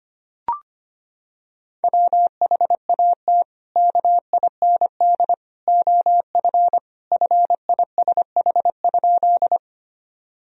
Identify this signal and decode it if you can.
Morse code, heard as a keyed tone, about 25 wpm, 700 Hz. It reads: WHAT KIND OF FISH?
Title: Sentences from Top 1000 Words - Review Part 1 - 25wpm